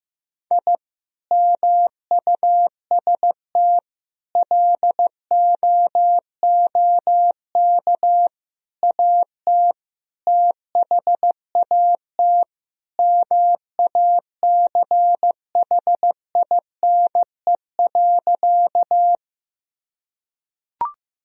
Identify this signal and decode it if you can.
Morse code, heard as a keyed tone, about 15 wpm, 700 Hz. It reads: I MUST LOOK AT THAT MACHINE.
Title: Sentences from Top 1000 Words - Review Part 2 - 15wpm